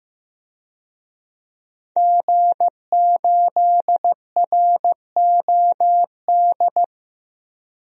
Morse code, heard as a keyed tone, about 15 wpm, 700 Hz. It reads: G8ROD
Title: DXCC-100 Call Signs - 5 Characters - Repeated 3x - 15wpm